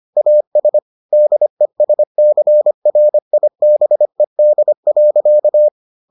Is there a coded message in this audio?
Yes: Morse code from 0.2 to 5.7 s, heard as a keyed tone, about 25 wpm, 600 Hz. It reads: AS DESCRIBED.